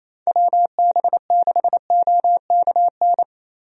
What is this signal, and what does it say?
Morse code, heard as a keyed tone, about 28 wpm, 700 Hz. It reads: WB6OKN
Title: USA Call Signs - 6 Characters - Repeated 3x - 28wpm